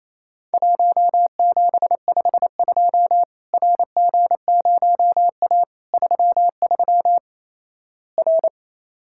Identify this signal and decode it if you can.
Morse code, heard as a keyed tone, about 28 wpm, 700 Hz. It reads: 1752 RG0A 33 R
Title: Worked All Europe DX Contest - QTC Exchange - 28wpm